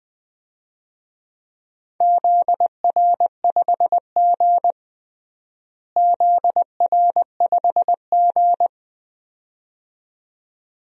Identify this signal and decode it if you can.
Morse code, heard as a keyed tone, about 20 wpm, 700 Hz. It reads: ZR5G ZR5G
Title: DXCC-100 Call Signs - 4 Characters - Repeated 2x - 20wpm